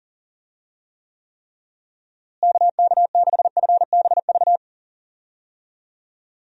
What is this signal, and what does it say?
Morse code, heard as a keyed tone, about 40 wpm, 700 Hz. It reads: KK6FBV
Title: USA Call Signs - 6 Characters - Repeated 3x - 40wpm